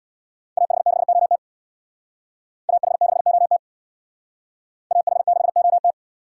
Morse code, heard as a keyed tone, about 61 wpm, 700 Hz. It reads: AH6OT AH6OT AH6OT
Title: Speed Racing - USA 2x2 Call Signs 50wpm